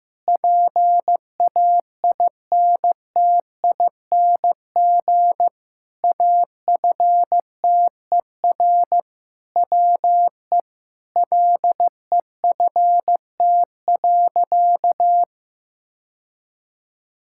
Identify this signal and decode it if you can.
Morse code, heard as a keyed tone, about 15 wpm, 700 Hz. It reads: PAINTING AFTER WE LEFT.
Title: Sentences from Top 1000 Words - Review Part 6 - 15wpm